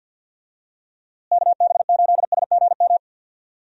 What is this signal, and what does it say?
Morse code, heard as a keyed tone, about 52 wpm, 700 Hz. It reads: KB8SGM
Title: USA Call Signs - 6 Characters - Repeated 3x - 50wpm